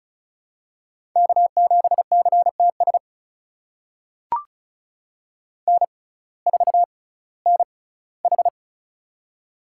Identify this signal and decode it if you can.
Morse code, heard as a keyed tone, about 36 wpm, spaced as for 19 wpm, 700 Hz. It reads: K7CTS N4NH
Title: USA Call Signs - 6x Character Spacing - 35wpm